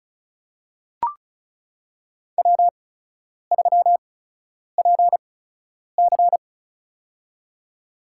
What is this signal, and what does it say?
Morse code, heard as a keyed tone, about 34 wpm, spaced as for 10 wpm, 700 Hz. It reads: W3PC
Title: USA Call Signs - 4 Characters - 8x Character Spacing - 35wpm